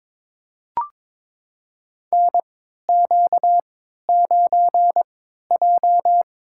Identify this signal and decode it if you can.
Morse code, heard as a keyed tone, about 22 wpm, spaced as for 13 wpm, 700 Hz. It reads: NQ9J
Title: USA Call Signs - 4 Characters - 3x Character Spacing - 22wpm